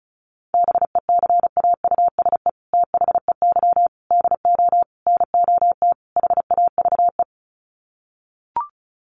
Morse code, heard as a keyed tone, about 34 wpm, 700 Hz. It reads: BECAUSE THEY DO NOT HAVE
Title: Sets of 5 Words Encore2 35wpm